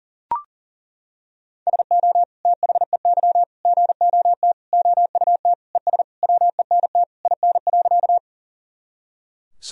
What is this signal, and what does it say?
Morse code, heard as a keyed tone, about 37 wpm, 700 Hz. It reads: SO THEY GOT OUT ES WENT IN.
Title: Sentences from Top 300 Words and Abbreviations 40wpm